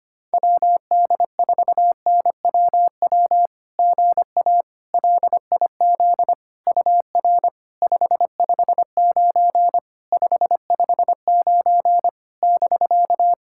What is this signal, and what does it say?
Morse code, heard as a keyed tone, about 25 wpm, 690 Hz. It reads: WD4NWW GA LIZ UR 559 559 <BK>
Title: SOTA QSO Practice 25wpm